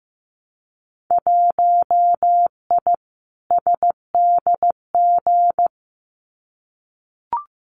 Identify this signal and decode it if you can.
Morse code, heard as a keyed tone, about 15 wpm, 700 Hz. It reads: 1I SDG
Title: Winter Field Day Exchanges 15wpm